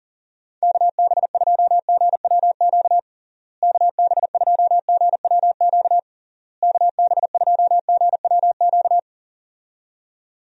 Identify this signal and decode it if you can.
Morse code, heard as a keyed tone, about 40 wpm, 700 Hz. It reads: KB2GWQ KB2GWQ KB2GWQ
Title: USA Call Signs - 6 Characters - Repeated 3x - 40wpm